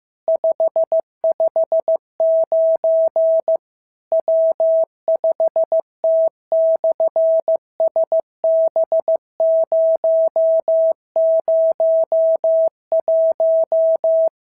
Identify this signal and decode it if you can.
Morse code, heard as a keyed tone, about 14 wpm, 650 Hz. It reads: 559 W5T/SB001
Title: SOTA QSO Practice 15wpm